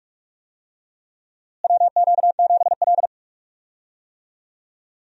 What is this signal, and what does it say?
Morse code, heard as a keyed tone, about 46 wpm, 700 Hz. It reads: WQ7L
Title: USA Call Signs - 4 Characters - Repeated 2x - 45wpm